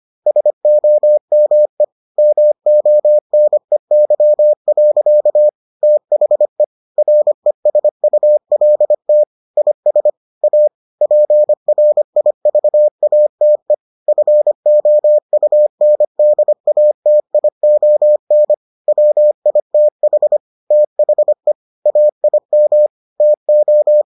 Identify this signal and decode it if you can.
Morse code, heard as a keyed tone, about 25 wpm, 600 Hz. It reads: SOME MONEY. THE RESULT IS A PRIVATE FOUNDATION WITH THE AIM TO